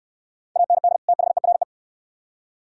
Morse code, heard as a keyed tone, about 62 wpm, 700 Hz. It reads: AND THERE